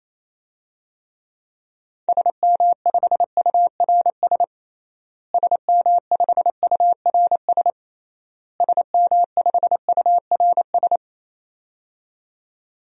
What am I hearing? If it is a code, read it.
Morse code, heard as a keyed tone, about 28 wpm, 700 Hz. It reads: SM5URS SM5URS SM5URS
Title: DXCC-100 Call Signs - 6 Characters - Repeated 3x - 28wpm